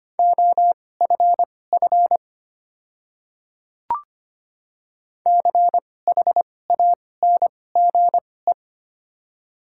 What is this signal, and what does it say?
Morse code, heard as a keyed tone, about 25 wpm, spaced as for 18 wpm, 700 Hz. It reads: OFF CHANGE
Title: 2x Character Spacing - Top 300 Words - 25wpm